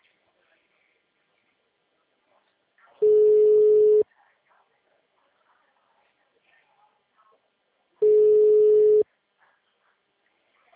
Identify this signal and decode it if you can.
Ringback tone (Europe, 425 Hz), with speech.